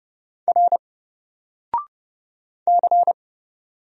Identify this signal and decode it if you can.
Morse code, heard as a keyed tone, about 29 wpm, 700 Hz. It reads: R C
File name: Single-Letter-Number-30wpm